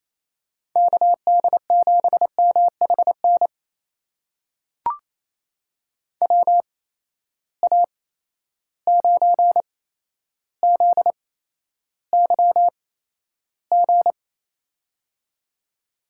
Morse code, heard as a keyed tone, about 28 wpm, spaced as for 12 wpm, 700 Hz. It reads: KD7MHN WA9ZYG
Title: USA Call Signs - 6 Characters - 8x Character Spacing - 28wpm